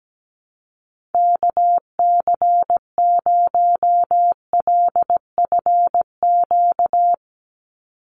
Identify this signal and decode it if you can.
Morse code, heard as a keyed tone, about 17 wpm, 700 Hz. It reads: KC0LFQ
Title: USA Call Signs 17wpm